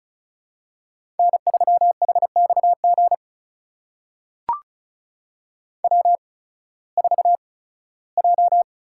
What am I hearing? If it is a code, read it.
Morse code, heard as a keyed tone, about 36 wpm, spaced as for 19 wpm, 700 Hz. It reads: N3HXG W4J
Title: USA Call Signs - 5 Characters - 8x Character Spacing - 35wpm